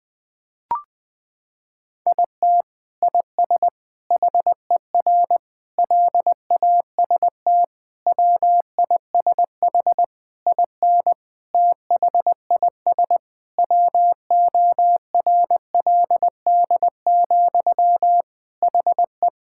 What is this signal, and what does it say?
Morse code, heard as a keyed tone, about 20 wpm, 700 Hz. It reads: IT IS HER LAST WISH IN THIS WORLD, HE
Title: Sentences from Top 1600 Words 20wpm